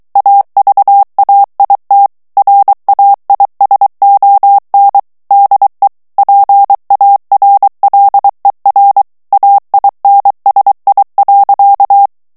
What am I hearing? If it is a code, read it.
Morse code, heard as a keyed tone, about 23 wpm, 800 Hz. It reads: AVAIT RAISON DE PARLER AINSI.
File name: morse mp3 / FRANCAIS TEXTE CLAIR 2 VITESSE 22 MOTS MINUTES